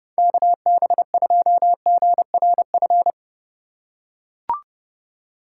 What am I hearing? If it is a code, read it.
Morse code, heard as a keyed tone, about 30 wpm, 700 Hz. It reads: KB2GRF
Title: USA Call Signs - 6 Characters - 2x Character Spacing - 30wpm